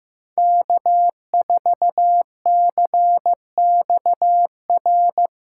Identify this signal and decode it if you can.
Morse code, heard as a keyed tone, about 18 wpm, 700 Hz. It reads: K4CXR
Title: Speed Racing - USA 1x3 Call Signs 15wpm